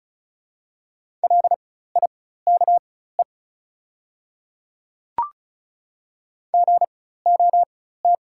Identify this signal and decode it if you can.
Morse code, heard as a keyed tone, about 36 wpm, spaced as for 16 wpm, 700 Hz. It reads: LIKE GOT
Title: 4x Character Spacing - Top 300 Words - 35wpm